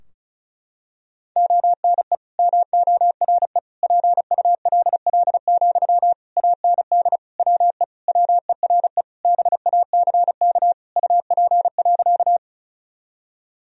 Morse code, heard as a keyed tone, about 36 wpm, 700 Hz. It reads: ONE MORE PULL, AND WE WERE BACK UP.
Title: Sentences from Top 1000 Words - Review Part 2 - 35wpm